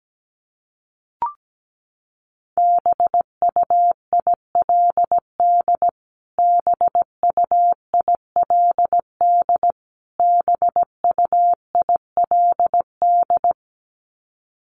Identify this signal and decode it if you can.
Morse code, heard as a keyed tone, about 17 wpm, 700 Hz. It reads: BUILD BUILD BUILD
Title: Top 500 Words Repeated 17wpm